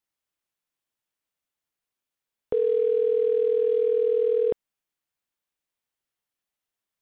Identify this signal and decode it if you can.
Ringback tone (North America), 440 + 480 Hz.